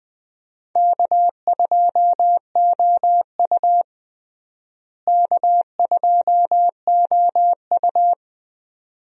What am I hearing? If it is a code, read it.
Morse code, heard as a keyed tone, about 21 wpm, 700 Hz. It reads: K2OU K2OU
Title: USA Call Signs - 4 Characters - Repeated 2x - 20wpm